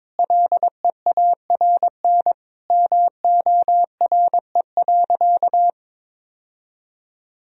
Morse code, heard as a keyed tone, about 22 wpm, 700 Hz. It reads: LEARN MORE.
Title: Sentences from Top 1000 Words - Review Part 4 - 22wpm